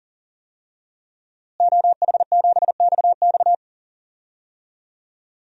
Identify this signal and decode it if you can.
Morse code, heard as a keyed tone, about 39 wpm, 700 Hz. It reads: OH7XX
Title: DXCC-100 Call Signs - 5 Characters - Repeated 2x - 40wpm